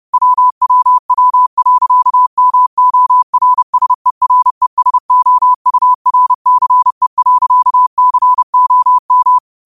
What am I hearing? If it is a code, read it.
Morse code, heard as a keyed tone, about 30 wpm, 1000 Hz. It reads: WWW.MORSERESOURCE.COM
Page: New quotes every day in morse code at 30 Words per minute.